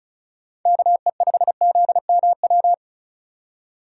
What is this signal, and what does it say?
Morse code, heard as a keyed tone, about 33 wpm, 700 Hz. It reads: KE5ZMW
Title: Speed Racing - USA 2x3 Call Signs 35wpm